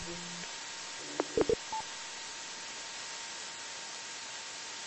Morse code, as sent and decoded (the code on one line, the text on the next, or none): none